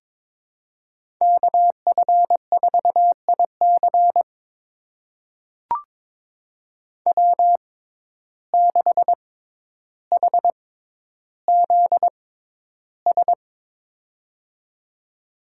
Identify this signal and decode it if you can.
Morse code, heard as a keyed tone, about 22 wpm, spaced as for 11 wpm, 700 Hz. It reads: KF4IC W6HZS
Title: USA Call Signs - 5 Characters - 6x Character Spacing - 22wpm